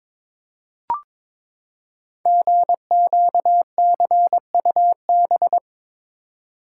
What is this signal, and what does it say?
Morse code, heard as a keyed tone, about 22 wpm, 700 Hz. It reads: GQCUB
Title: Code Groups - 5 Letter:Numbers - 22wpm